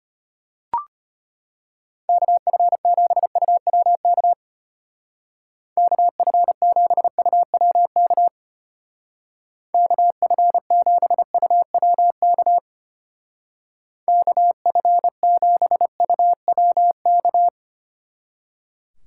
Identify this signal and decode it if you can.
Morse code, heard as a keyed tone, about 32 wpm, 700 Hz. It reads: KF7UWK KF7UWK KF7UWK KF7UWK